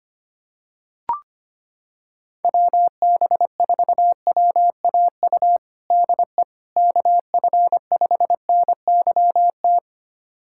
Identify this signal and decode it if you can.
Morse code, heard as a keyed tone, about 25 wpm, 700 Hz. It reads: WB4WAU DE KF5NYT
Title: USA Callsign DE Callsign - 6 Characters - 25wpm